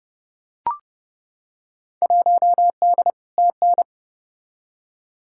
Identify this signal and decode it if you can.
Morse code, heard as a keyed tone, about 32 wpm, 700 Hz. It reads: 1D TN
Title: Field Day Exchanges - 30wpm